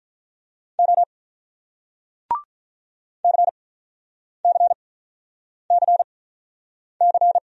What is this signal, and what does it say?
Morse code, heard as a keyed tone, about 48 wpm, spaced as for 9 wpm, 700 Hz. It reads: O CCCC